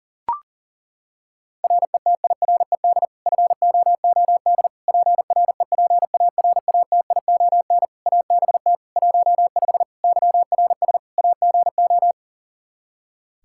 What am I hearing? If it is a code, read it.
Morse code, heard as a keyed tone, about 40 wpm, 700 Hz. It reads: RETIRED FOOD PREPARATION ABT 15 YRS AGO
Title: K1USN SST QSO Practice 40wpm